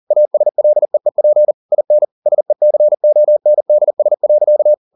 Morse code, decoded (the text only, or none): ASLEEP IN SECONDS.